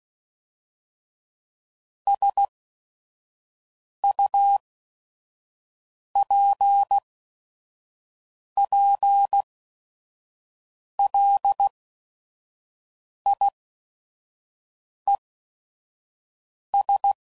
Morse code, heard as a keyed tone, about 16 wpm, spaced as for 5 wpm, 800 Hz. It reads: SUPPLIES
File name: CW Code Groups @5WPM / Technical Subjects / SwPwrSup